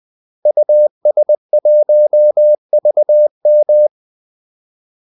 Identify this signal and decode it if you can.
Morse code, heard as a keyed tone, about 20 wpm, 600 Hz. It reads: US1VM
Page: Kaldesignaler 20 wpm